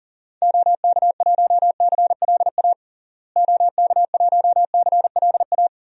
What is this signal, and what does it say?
Morse code, heard as a keyed tone, about 40 wpm, 700 Hz. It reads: OK1CLA OK1CLA
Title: DXCC-100 Call Signs - 6 Characters - Repeated 3x - 40wpm